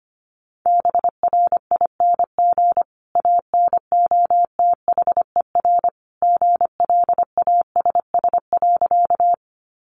Morse code, heard as a keyed tone, about 25 wpm, 700 Hz. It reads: BRING ANOTHER GLASS.
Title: Sentences from Top 1300 Words 25wpm